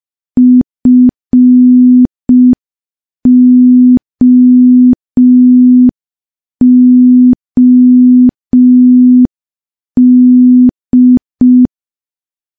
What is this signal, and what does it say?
Morse code, heard as a keyed tone, about 5 wpm, 250 Hz. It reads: FOOD
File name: morse / morse-training-05WPM